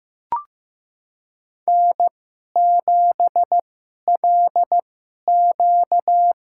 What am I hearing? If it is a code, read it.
Morse code, heard as a keyed tone, about 15 wpm, spaced as for 11 wpm, 700 Hz. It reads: N7LQ
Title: USA Call Signs - 4 Characters - 2x Character Spacing - 15wpm